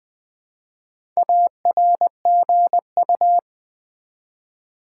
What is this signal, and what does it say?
Morse code, heard as a keyed tone, about 20 wpm, 700 Hz. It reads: ARGU
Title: Build A Word - 1 Letter At A Time - 3 to 5-Letter Words - 20wpm